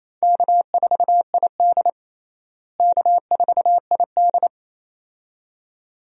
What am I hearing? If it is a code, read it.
Morse code, heard as a keyed tone, about 28 wpm, 700 Hz. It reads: K4ID K4ID
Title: USA Call Signs - 4 Characters - Repeated 2x - 28wpm